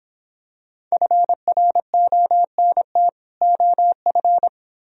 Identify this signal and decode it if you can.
Morse code, heard as a keyed tone, about 26 wpm, 700 Hz. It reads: FRONT OF